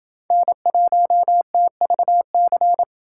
Morse code, heard as a keyed tone, about 29 wpm, 700 Hz. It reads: N1TVC